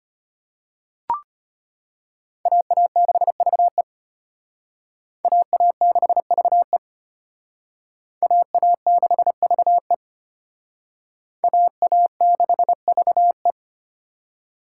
Morse code, heard as a keyed tone, about 32 wpm, 700 Hz. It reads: AA6VE AA6VE AA6VE AA6VE